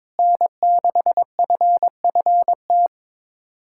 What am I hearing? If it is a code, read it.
Morse code, heard as a keyed tone, about 22 wpm, 700 Hz. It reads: N6FFT
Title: USA Callsign DE Callsign - 5 Characters - 22wpm